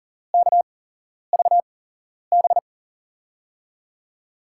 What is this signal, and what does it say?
Morse code, heard as a keyed tone, about 40 wpm, spaced as for 11 wpm, 700 Hz. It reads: KVB
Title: USA Call Signs - 6 Characters - 8x Character Spacing - 40wpm